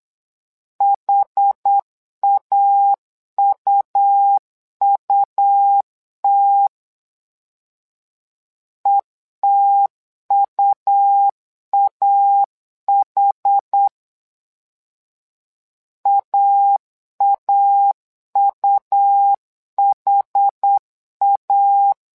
Morse code, heard as a keyed tone, about 8 wpm, 800 Hz. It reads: HAUUT ETUAH AAUHA
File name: LEZIONI / 2 UHDN / 2 UH 8wpm 2129